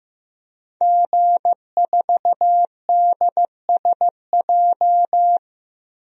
Morse code, heard as a keyed tone, about 15 wpm, 700 Hz. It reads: G4DSJ
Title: DXCC-100 Call Signs - 5 Characters - 15wpm